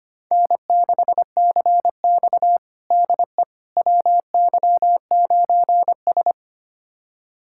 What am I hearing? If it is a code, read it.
Morse code, heard as a keyed tone, about 25 wpm, 700 Hz. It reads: N6CX DE WY9S
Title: USA Callsign DE Callsign - 4 Characters - 25wpm